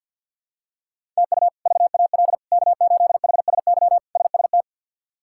Morse code, heard as a keyed tone, about 51 wpm, 700 Hz. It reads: TU VAL K8HSY SST
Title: K1USN SST QSO Practice - 50wpm